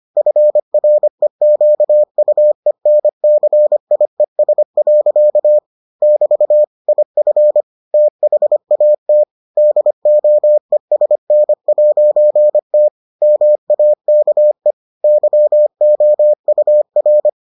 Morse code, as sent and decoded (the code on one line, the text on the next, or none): ..-. .-. --.- ..- . -. -.-. .. ... .-.-.- / -...- / .. ..-. / - .... .- - / -.. --- . ... -. .----. - / -- .- -.- . / -.-- --- ..- .-.
FREQUENCIES. <BT> IF THAT DOESN'T MAKE YOUR